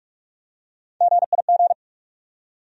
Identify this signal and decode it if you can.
Morse code, heard as a keyed tone, about 43 wpm, 700 Hz. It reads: GIG
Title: USA Call Sign Postfixes - 3 Letters 45wpm